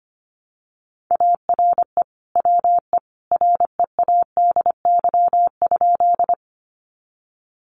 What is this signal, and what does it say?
Morse code, heard as a keyed tone, about 25 wpm, 700 Hz. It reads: ARE WE READY?
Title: Sentences from Top 400 Words 25wpm